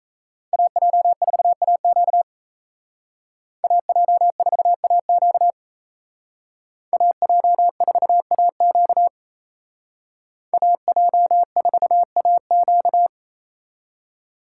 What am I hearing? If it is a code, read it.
Morse code, heard as a keyed tone, about 36 wpm, 700 Hz. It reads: AJ4AQ AJ4AQ AJ4AQ AJ4AQ